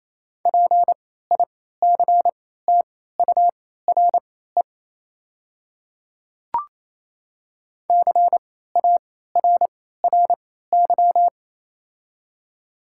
Morse code, heard as a keyed tone, about 28 wpm, spaced as for 16 wpm, 700 Hz. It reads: PICTURE CARRY
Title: 3x Character Spacing - Top 300 Words - 28wpm